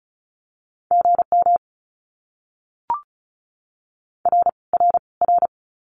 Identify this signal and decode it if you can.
Morse code, heard as a keyed tone, about 34 wpm, 700 Hz. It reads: GM R R R
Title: Abbreviations Repeated 3x - 35wpm